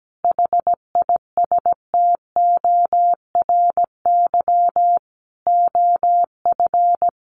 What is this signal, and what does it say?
Morse code, heard as a keyed tone, about 17 wpm, 700 Hz. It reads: HISTORY OF
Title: Sets of 4 Words 17wpm